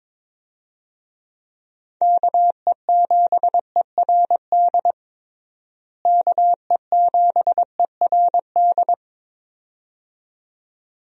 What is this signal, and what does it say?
Morse code, heard as a keyed tone, about 21 wpm, 700 Hz. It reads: KE7ERD KE7ERD
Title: USA Call Signs - 6 Characters - Repeated 2x - 22wpm